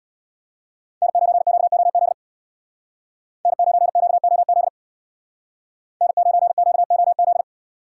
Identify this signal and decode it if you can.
Morse code, heard as a keyed tone, about 64 wpm, 700 Hz. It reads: N9QOZ N9QOZ N9QOZ